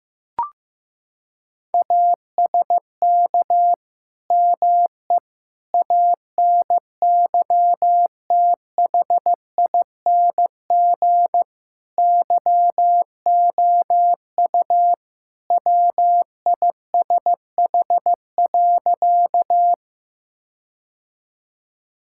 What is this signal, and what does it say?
Morse code, heard as a keyed tone, about 16 wpm, 700 Hz. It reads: ASK ME ANYTHING YOU WISH.
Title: Sentences from Top 1600 Words 15wpm